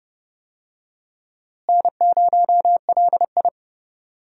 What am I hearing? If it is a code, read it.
Morse code, heard as a keyed tone, about 30 wpm, 700 Hz. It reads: N0LI
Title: USA Call Signs - 4 Characters - Repeated 3x - 30wpm